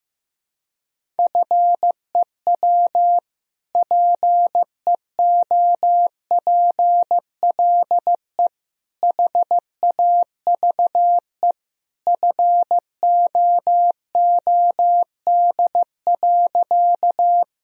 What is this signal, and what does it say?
Morse code, heard as a keyed tone, about 15 wpm, 700 Hz. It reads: FEW PEOPLE HAVE FOOD.
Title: Sentences from Top 1000 Words - Review Part 1 - 15wpm